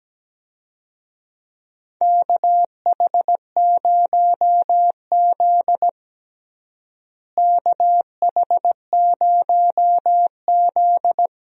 Morse code, heard as a keyed tone, about 17 wpm, 700 Hz. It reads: KH0Z KH0Z
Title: USA Call Signs - 4 Characters - Repeated 2x - 17wpm